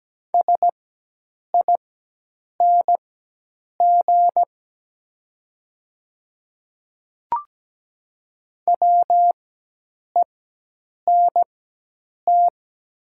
Morse code, heard as a keyed tone, about 17 wpm, spaced as for 8 wpm, 700 Hz. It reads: SING WENT
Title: 4x Character Spacing - Top 500 Words - 17wpm